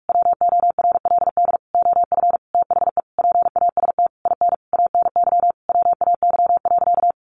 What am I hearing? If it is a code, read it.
Morse code, heard as a keyed tone, about 44 wpm, 700 Hz. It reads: WORLD OF THE PAST IN ANY WAY.
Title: A Sound Of Thunder 45wpm